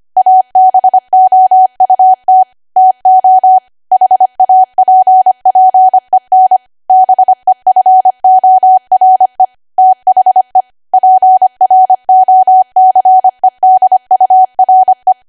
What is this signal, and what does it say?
Morse code, heard as a keyed tone, about 25 wpm, 750 Hz. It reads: ABOUT TO HAPPEN BEFORE THE PROCEDURE
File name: Archive / 25 WPM / 25WPM